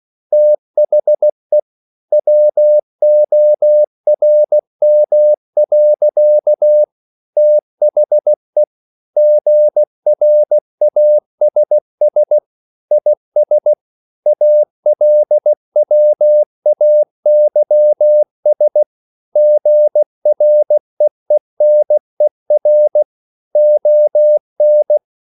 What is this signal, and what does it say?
Morse code, heard as a keyed tone, about 16 wpm, 600 Hz. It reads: THE WORM. THE GRASS IS ALWAYS GREENER ON
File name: CW / Practice / EnglishSayings16